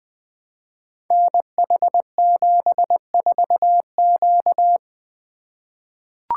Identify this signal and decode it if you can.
Morse code, heard as a keyed tone, about 21 wpm, 700 Hz. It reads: NH74Q
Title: Code Groups - 5 Letter:Numbers - 20wpm